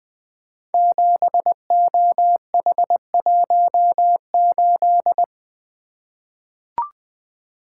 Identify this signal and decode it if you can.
Morse code, heard as a keyed tone, about 20 wpm, 700 Hz. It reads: 7OH18
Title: Code Groups - 5 Letter:Numbers - 20wpm